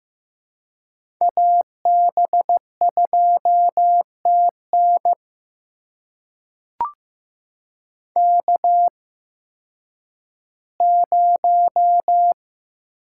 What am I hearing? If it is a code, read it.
Morse code, heard as a keyed tone, about 15 wpm, spaced as for 10 wpm, 700 Hz. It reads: AB2TN K0
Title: USA Call Signs - 8x Character Spacing - 15wpm